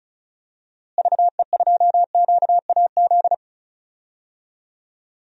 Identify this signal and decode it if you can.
Morse code, heard as a keyed tone, about 36 wpm, 700 Hz. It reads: VE2QAZ
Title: DXCC-100 Call Signs - 6 Characters - Repeated 2x - 35wpm